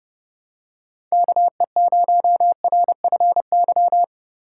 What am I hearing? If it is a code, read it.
Morse code, heard as a keyed tone, about 29 wpm, 700 Hz. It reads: KE0RFY
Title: USA Call Signs - 4x Character Spacing - 30wpm